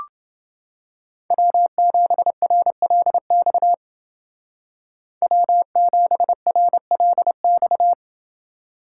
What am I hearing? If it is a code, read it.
Morse code, heard as a keyed tone, about 29 wpm, 700 Hz. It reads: W7RLX W7RLX